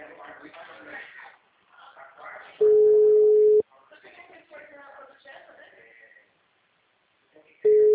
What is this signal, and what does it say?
Ringback tone (Europe, 425 Hz), with speech.